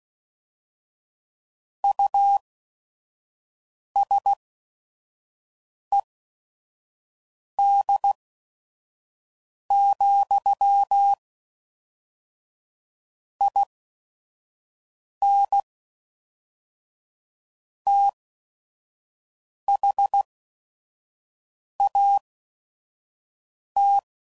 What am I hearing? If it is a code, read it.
Morse code, heard as a keyed tone, about 16 wpm, spaced as for 5 wpm, 800 Hz. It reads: USED, IN THAT
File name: CW Code Groups @5WPM